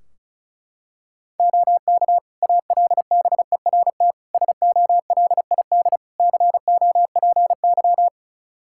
Morse code, heard as a keyed tone, about 34 wpm, 700 Hz. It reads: OK ALBERT SOLID COPY
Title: Practice QSO - Exchange 3 - More Info - 35wpm